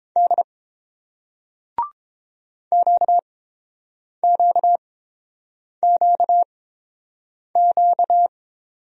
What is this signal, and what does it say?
Morse code, heard as a keyed tone, about 30 wpm, spaced as for 7 wpm, 700 Hz. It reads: D QQQQ